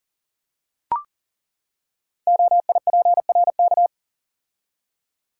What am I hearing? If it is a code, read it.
Morse code, heard as a keyed tone, about 41 wpm, 690 Hz. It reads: OIPRK
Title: Code Groups - 5 Letters - 40wpm